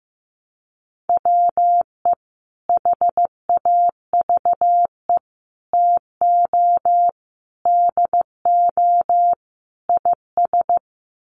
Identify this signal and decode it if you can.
Morse code, heard as a keyed tone, about 15 wpm, 700 Hz. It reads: WE HAVE TO DO IS
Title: Sets of 5 Words Encore 15wpm